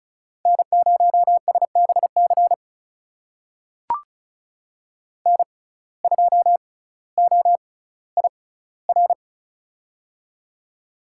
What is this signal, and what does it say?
Morse code, heard as a keyed tone, about 36 wpm, spaced as for 18 wpm, 700 Hz. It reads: N0SBC N2OIR
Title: USA Call Signs - 5 Characters - 6x Character Spacing - 35wpm